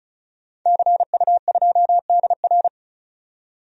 Morse code, heard as a keyed tone, about 36 wpm, 700 Hz. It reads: CU2DR
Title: DXCC-100 Call Signs - 5 Characters - 35wpm